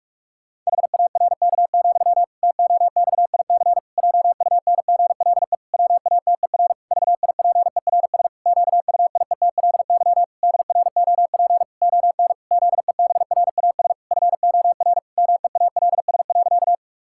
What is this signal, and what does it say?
Morse code, heard as a keyed tone, about 41 wpm, 700 Hz. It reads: HARK, TOXIC JUNGLE WATER VIPERS QUIETLY DROP ON ZEBRAS FOR MEALS.
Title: Pangram Challenge v2 - 45wpm